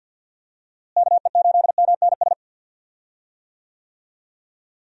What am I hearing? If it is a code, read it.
Morse code, heard as a keyed tone, about 49 wpm, 700 Hz. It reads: KE8MNS
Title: USA Call Signs - 6 Characters - Repeated 3x - 50wpm